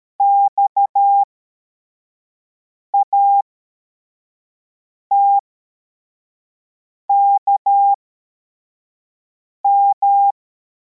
Morse code, heard as a keyed tone, about 13 wpm, spaced as for 4 wpm, 800 Hz. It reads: XATKM